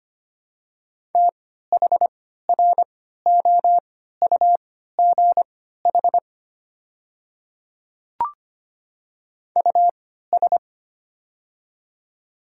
Morse code, heard as a keyed tone, about 25 wpm, spaced as for 14 wpm, 700 Hz. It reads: THROUGH US
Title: 3x Character Spacing - Top 300 Words - 25wpm